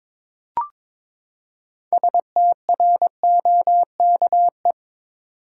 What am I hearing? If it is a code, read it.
Morse code, heard as a keyed tone, about 22 wpm, 700 Hz. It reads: STROKE
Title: Six Letter Words - 22wpm